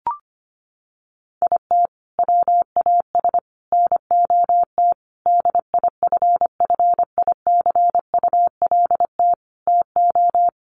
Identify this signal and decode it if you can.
Morse code, heard as a keyed tone, about 25 wpm, 700 Hz. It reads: IT WAS NOT DIFFICULT TO
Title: Sentences from Top 1500 Words 25wpm